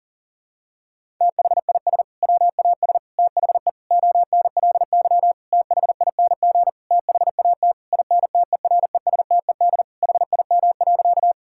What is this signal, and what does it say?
Morse code, heard as a keyed tone, about 37 wpm, 700 Hz. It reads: THIS WAS THE ONLY THING THAT INTERESTED HIM.
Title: Sentences from Top 1000 Words - Review Part 6 - 40wpm